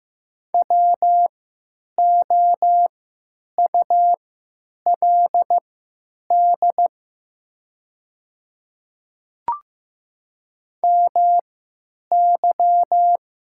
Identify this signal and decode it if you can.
Morse code, heard as a keyed tone, about 15 wpm, spaced as for 9 wpm, 690 Hz. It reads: WOULD MY
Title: 3x Character Spacing - Top 100 Words - 15wpm